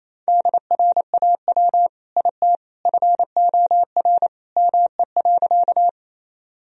Morse code, heard as a keyed tone, about 28 wpm, 700 Hz. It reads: DRAW IT FOR ME.
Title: Sentences from Top 1000 Words - Review Part 1 - 28wpm